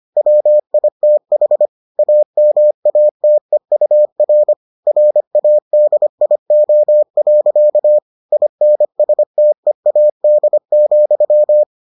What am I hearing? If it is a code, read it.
Morse code, heard as a keyed tone, about 25 wpm, 600 Hz. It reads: WITH AMATEUR RADIO. INSTEAD,